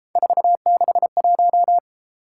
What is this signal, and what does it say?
Morse code, heard as a keyed tone, about 32 wpm, 700 Hz. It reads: 461